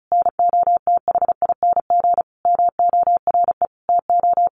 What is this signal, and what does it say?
Morse code, heard as a keyed tone, about 34 wpm, 700 Hz. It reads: NOTHING MORE TO
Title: Sentences from Top 1100 Words 35wpm